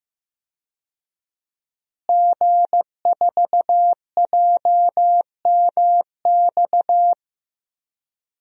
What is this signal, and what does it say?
Morse code, heard as a keyed tone, about 15 wpm, 700 Hz. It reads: G4JMX
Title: DXCC-100 Call Signs - 5 Characters - Repeated 3x - 15wpm